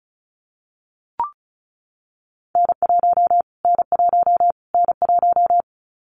Morse code, heard as a keyed tone, about 34 wpm, 700 Hz. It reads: N1 N1 N1
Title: Callsign Prefixes for Top 25 DXCC - Repeated 3x - 35wpm